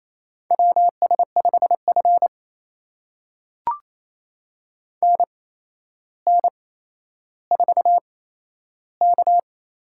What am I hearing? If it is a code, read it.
Morse code, heard as a keyed tone, about 28 wpm, spaced as for 12 wpm, 700 Hz. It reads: WS5F NN4K
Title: USA Call Signs - 4 Characters - 8x Character Spacing - 28wpm